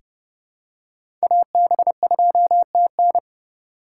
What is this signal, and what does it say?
Morse code, heard as a keyed tone, about 32 wpm, 690 Hz. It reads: AB2TN
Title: USA Call Signs - 5 Characters - 6x Character Spacing - 30wpm